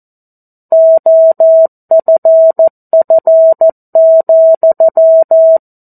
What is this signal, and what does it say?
Morse code, heard as a keyed tone, about 14 wpm, 650 Hz. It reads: OFF,
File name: 02-Chapter-02 14 WPM